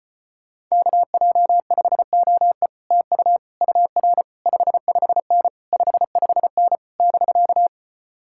Morse code, heard as a keyed tone, about 33 wpm, 700 Hz. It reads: KJ5OE TU UR 55N 55N <BK>